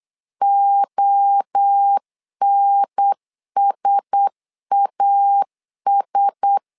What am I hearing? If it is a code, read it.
Morse code, heard as a keyed tone, about 8 wpm, 800 Hz. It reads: ONSAS
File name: LEZIONI / 4 Q1S 8wpm 2710